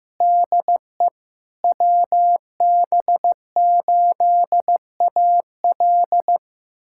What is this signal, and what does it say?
Morse code, heard as a keyed tone, about 15 wpm, 700 Hz. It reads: DE WB8AL
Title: USA Callsign DE Callsign 15wpm